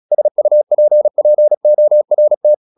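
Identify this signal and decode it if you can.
Morse code, heard as a keyed tone, about 36 wpm, 600 Hz. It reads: SUPPORT
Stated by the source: Morse practice at 36 wpm, text of Ken Follett's novel Never, from CW med Gnister.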